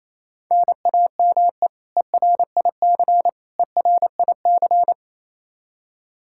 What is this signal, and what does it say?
Morse code, heard as a keyed tone, about 28 wpm, 700 Hz. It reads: NAME ERIC ERIC
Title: Practice QSO - Exchange 1 - Beginning - 28wpm